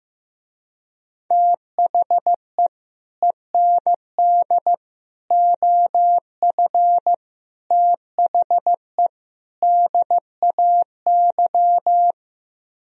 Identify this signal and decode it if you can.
Morse code, heard as a keyed tone, about 16 wpm, 700 Hz. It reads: THE END OF THE DAY
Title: Speed-Racing - Sets of 5 Words 15wpm